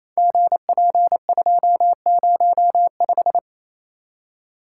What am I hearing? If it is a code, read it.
Morse code, heard as a keyed tone, about 28 wpm, 700 Hz. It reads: GP205
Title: Code Groups - 5 Letter:Numbers - 28wpm